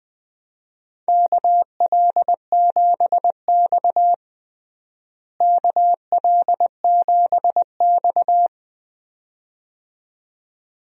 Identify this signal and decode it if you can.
Morse code, heard as a keyed tone, about 21 wpm, 700 Hz. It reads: KL7X KL7X
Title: DXCC-100 Call Signs - 4 Characters - Repeated 2x - 20wpm